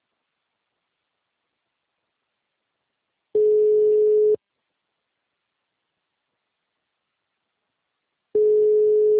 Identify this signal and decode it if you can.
Ringback tone (Europe, 425 Hz).
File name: out- -2030- - - .32